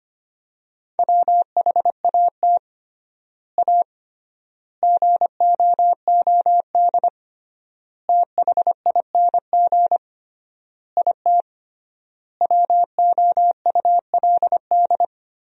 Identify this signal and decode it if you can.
Morse code, heard as a keyed tone, about 25 wpm, 700 Hz. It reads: WHAT A GOOD THING IT WOULD
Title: 3x Word-Spacing - Sentences from Top 200 Words - 25wpm